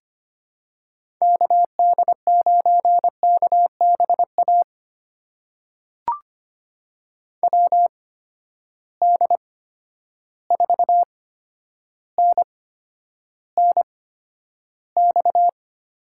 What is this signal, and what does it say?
Morse code, heard as a keyed tone, about 25 wpm, spaced as for 11 wpm, 700 Hz. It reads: KD9KBA WD4NNX
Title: USA Call Signs - 6 Characters - 8x Character Spacing - 25wpm